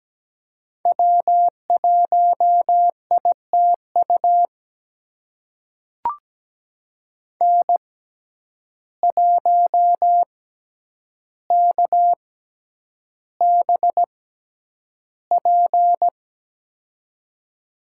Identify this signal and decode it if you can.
Morse code, heard as a keyed tone, about 17 wpm, spaced as for 9 wpm, 700 Hz. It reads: W1ITU N1KBP
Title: USA Call Signs - 5 Characters - 6x Character Spacing - 17wpm